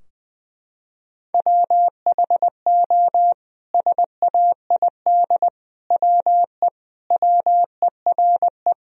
Morse code, heard as a keyed tone, about 20 wpm, 700 Hz. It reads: WHO SAID WE WERE
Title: Sentences from Top 1000 Words - Review Part 4 - 20wpm